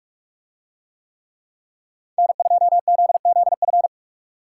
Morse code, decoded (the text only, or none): N2ZZF